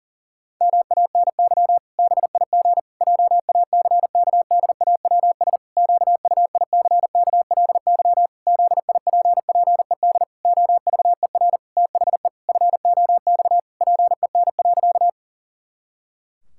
Morse code, heard as a keyed tone, about 40 wpm, 700 Hz. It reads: MANY BIG JACKDAWS QUICKLY ZIPPED OVER THE FOX PEN.
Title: Pangram Challenge v2 - 40wpm